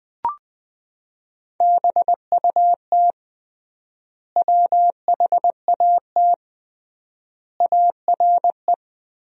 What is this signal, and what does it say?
Morse code, heard as a keyed tone, about 20 wpm, 700 Hz. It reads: BUT WHAT ARE
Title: 3x Word-Spacing - Sentences from Top 100 Words - 20wpm